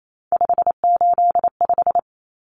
Morse code, heard as a keyed tone, about 28 wpm, 700 Hz. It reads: 585
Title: Common Cut Numbers - 3-Digit 28wpm